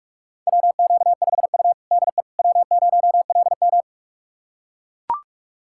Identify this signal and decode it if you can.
Morse code, heard as a keyed tone, about 41 wpm, 700 Hz. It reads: WQ5U DE W0LM
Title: USA Callsign DE Callsign - 4 Characters - 45wpm